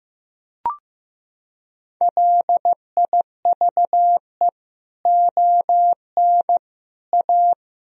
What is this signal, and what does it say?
Morse code, heard as a keyed tone, about 15 wpm, 700 Hz. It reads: LIVE ON A
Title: Sentences from Top 1000 Words - Review Part 1 - 15wpm